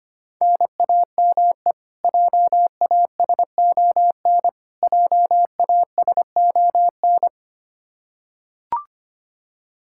Morse code, heard as a keyed tone, about 25 wpm, 700 Hz. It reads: NAME JASON JASON
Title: K1USN SST QSO Practice 25wpm